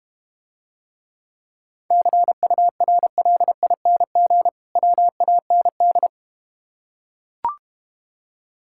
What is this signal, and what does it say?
Morse code, heard as a keyed tone, about 32 wpm, 700 Hz. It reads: CURLING WAND